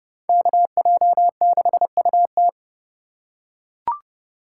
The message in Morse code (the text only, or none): KJ6UT